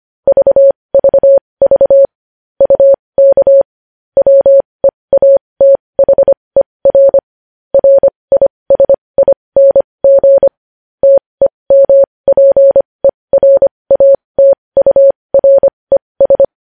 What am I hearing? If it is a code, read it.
Morse code, heard as a keyed tone, about 25 wpm, spaced as for 20 wpm, 550 Hz. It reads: VVV UK WEATHER RISING TEMPERATURES
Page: News Headlines in Morse Code at 20 WPM